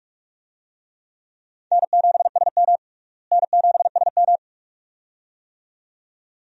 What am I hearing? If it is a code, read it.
Morse code, heard as a keyed tone, about 41 wpm, 700 Hz. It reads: N7SM N7SM
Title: USA Call Signs - 4 Characters - Repeated 2x - 45wpm